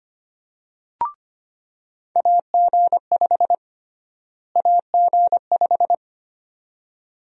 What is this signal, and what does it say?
Morse code, heard as a keyed tone, about 25 wpm, 700 Hz. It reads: AG5 AG5
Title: USA Call Sign Prefixes - Repeated 2x - 25wpm